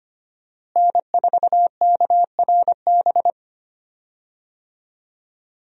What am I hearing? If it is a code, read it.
Morse code, heard as a keyed tone, about 25 wpm, 700 Hz. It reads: N4KRB
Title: USA Call Signs - 5 Characters - Repeated 2x - 25wpm